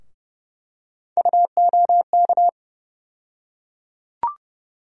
Morse code, heard as a keyed tone, about 32 wpm, 700 Hz. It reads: UOK
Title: USA Call Sign Postfixes - 3 Letters 30wpm